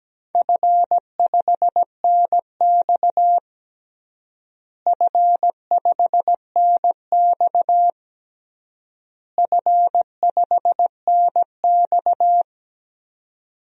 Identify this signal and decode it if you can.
Morse code, heard as a keyed tone, about 17 wpm, 700 Hz. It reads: F5NX F5NX F5NX
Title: DXCC-100 Call Signs - 4 Characters - Repeated 3x - 17wpm